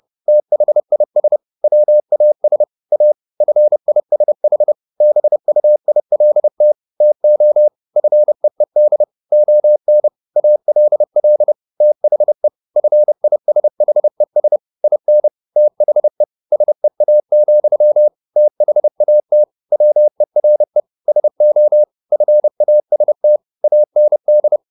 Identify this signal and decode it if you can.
Morse code, heard as a keyed tone, about 30 wpm, 600 Hz. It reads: THIS WAS A FISH BUILT TO FEED ON ALL THE FISHES IN THE SEA, THAT WERE SO FAST AND